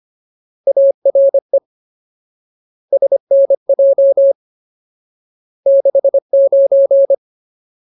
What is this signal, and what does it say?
Morse code, heard as a keyed tone, about 25 wpm, 550 Hz. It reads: ARE SNJ 69
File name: Ultimate Word List / 25WPM Ultimate